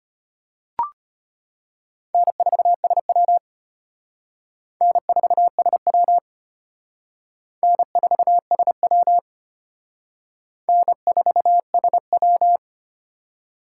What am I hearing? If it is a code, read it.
Morse code, heard as a keyed tone, about 32 wpm, 700 Hz. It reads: N4SW N4SW N4SW N4SW